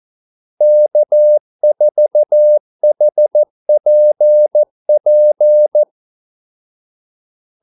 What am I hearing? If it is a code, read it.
Morse code, heard as a keyed tone, about 14 wpm, 600 Hz. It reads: K4HPP
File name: Callsigns-14wpm